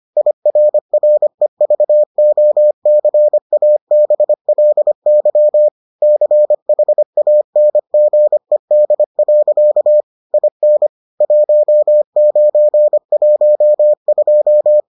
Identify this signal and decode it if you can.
Morse code, heard as a keyed tone, about 25 wpm, 600 Hz. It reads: IRREVOCABLY CHANGED. IN 1912